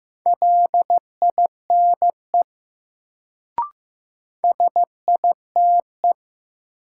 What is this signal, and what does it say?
Morse code, heard as a keyed tone, about 15 wpm, 700 Hz. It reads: LINE SITE
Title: Sentences from Top 1000 Words - Review Part 1 - New Word Drill - 15wpm